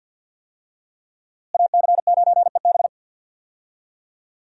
Morse code, heard as a keyed tone, about 51 wpm, 700 Hz. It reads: AC9EB
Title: USA Call Signs - 5 Characters - Repeated 3x - 50wpm